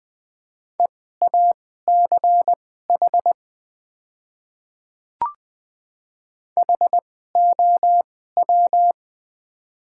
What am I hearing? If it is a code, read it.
Morse code, heard as a keyed tone, about 20 wpm, spaced as for 14 wpm, 690 Hz. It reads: EACH HOW
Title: 2x Character Spacing - Top 100 Words - 20wpm